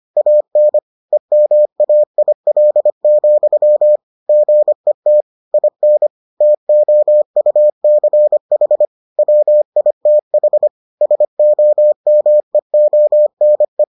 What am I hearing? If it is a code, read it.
Morse code, heard as a keyed tone, about 25 wpm, 600 Hz. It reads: AN EMAIL, GET IN TOUCH WITH SOMEONE